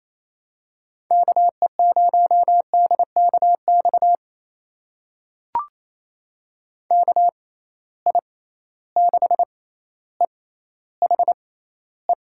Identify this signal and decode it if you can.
Morse code, heard as a keyed tone, about 28 wpm, spaced as for 14 wpm, 700 Hz. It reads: KE0DKX KI6EHE
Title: USA Call Signs - 6 Characters - 6x Character Spacing - 28wpm